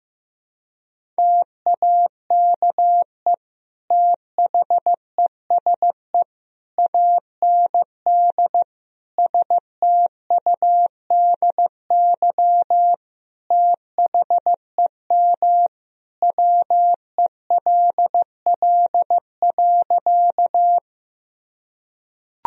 Morse code, heard as a keyed tone, about 15 wpm, 700 Hz. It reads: TAKE THESE AND STUDY THEM WELL.
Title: Sentences from Top 1000 Words - Review Part 1 - 15wpm